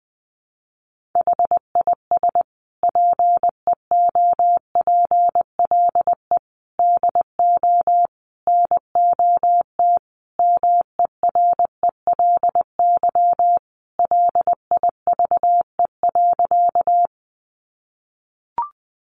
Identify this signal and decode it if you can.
Morse code, heard as a keyed tone, about 20 wpm, 700 Hz. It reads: HIS PEOPLE DO NOT MERELY LIVE.
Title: Sentences from Top 1400 Words 20wpm